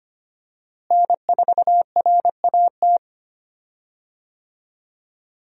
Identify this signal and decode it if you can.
Morse code, heard as a keyed tone, about 25 wpm, 700 Hz. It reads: N4RAT
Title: USA Call Signs - 5 Characters - Repeated 2x - 25wpm